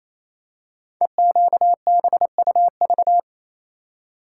Morse code, heard as a keyed tone, about 27 wpm, 690 Hz. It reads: EQBUV
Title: Code Groups - 5 Letters - 28wpm